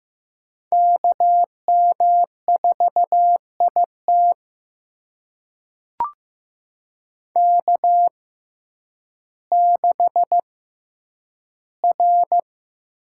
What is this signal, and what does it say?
Morse code, heard as a keyed tone, about 15 wpm, spaced as for 9 wpm, 700 Hz. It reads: KM4IT K6R
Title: USA Call Signs - 5 Characters - 6x Character Spacing - 15wpm